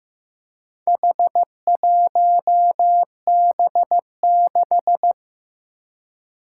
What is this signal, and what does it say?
Morse code, heard as a keyed tone, about 15 wpm, 700 Hz. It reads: H1B6
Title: Four Challenging Letter-Number Combinations 15wpm